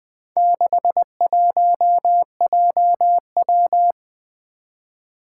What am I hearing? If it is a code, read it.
Morse code, heard as a keyed tone, about 21 wpm, 700 Hz. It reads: 61JW
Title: Four Challenging Letter-Number Combinations 20wpm